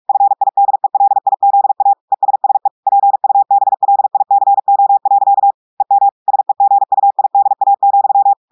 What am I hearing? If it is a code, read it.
Morse code, heard as a keyed tone, about 46 wpm, 800 Hz. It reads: FIDELIZA ESSE PUBLICO. EM SEGUIDA,
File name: audios / portugais / portugais texte en clair 2 vitesse 45 mots minutes